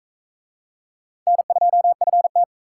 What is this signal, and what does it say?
Morse code, heard as a keyed tone, about 37 wpm, 700 Hz. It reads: N2FT